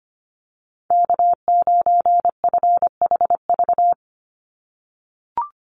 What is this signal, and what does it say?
Morse code, heard as a keyed tone, about 25 wpm, 700 Hz. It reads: K9FHV
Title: USA Call Signs 25wpm